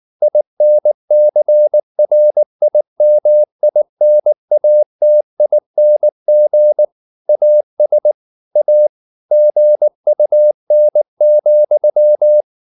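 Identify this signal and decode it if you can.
Morse code, heard as a keyed tone, about 19 wpm, 600 Hz. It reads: INCRIMINATING AS A GUN,
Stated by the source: Morse code at 19 wpm from CW med Gnister, reading Ken Follett's novel Never in English.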